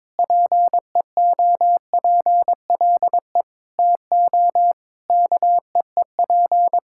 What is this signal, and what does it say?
Morse code, heard as a keyed tone, about 21 wpm, 700 Hz. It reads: PEOPLE TO KEEP
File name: Sentences with 9-Word Phrases - 22wpm